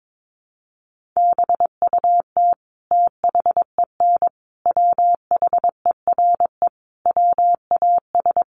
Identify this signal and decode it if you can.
Morse code, heard as a keyed tone, about 22 wpm, 700 Hz. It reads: BUT THEN WHERE WAS
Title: Sentences from Top 1300 Words 22wpm